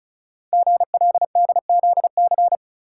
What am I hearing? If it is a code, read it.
Morse code, heard as a keyed tone, about 36 wpm, 690 Hz. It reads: GLDZC
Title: Code Groups - 5 Letters - 35wpm